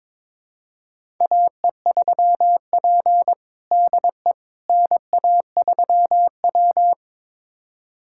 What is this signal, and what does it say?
Morse code, heard as a keyed tone, about 22 wpm, 700 Hz. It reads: AE3P DE NA3W
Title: USA Callsign DE Callsign - 4 Characters - 22wpm